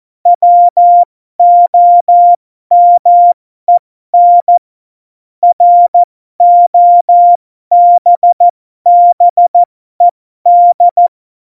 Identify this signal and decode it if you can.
Morse code, heard as a keyed tone, about 14 wpm, spaced as for 12 wpm, 700 Hz. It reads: WOMEN ROBBED
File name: Morsekurs / Witze